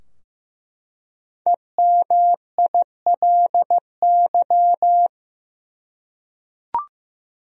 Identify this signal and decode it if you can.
Morse code, heard as a keyed tone, about 15 wpm, 700 Hz. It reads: EMILY
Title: USA Common Names - 15wpm